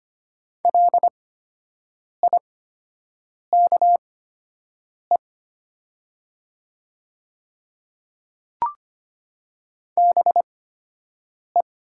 Morse code, heard as a keyed tone, about 25 wpm, spaced as for 7 wpm, 700 Hz. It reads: LIKE BE
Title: 8x Character Spacing - Top 100 Words - 25wpm